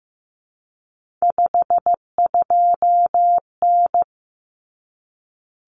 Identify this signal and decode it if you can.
Morse code, heard as a keyed tone, about 15 wpm, 700 Hz. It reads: 52N
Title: Common Cut Numbers - 3-Digit 15wpm